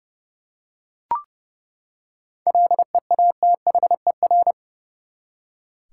Morse code, heard as a keyed tone, about 31 wpm, 700 Hz. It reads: LEATHER
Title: Seven Letter Words - 30wpm